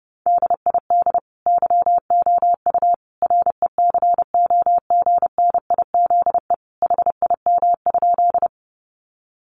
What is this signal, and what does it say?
Morse code, heard as a keyed tone, about 30 wpm, 700 Hz. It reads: DID YOU RECOGNIZE HIM?
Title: Sentences from Top 1400 Words 30wpm